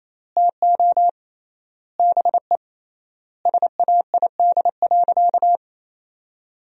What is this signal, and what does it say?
Morse code, heard as a keyed tone, about 27 wpm, 700 Hz. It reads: TO BE SAID.
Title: 3x Word-Spacing - Sentences from Top 500 Words - 28wpm